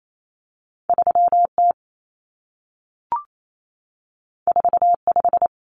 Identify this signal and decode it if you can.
Morse code, heard as a keyed tone, about 28 wpm, 700 Hz. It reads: 3T 45
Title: Common Cut Numbers - 2-Digit 28wpm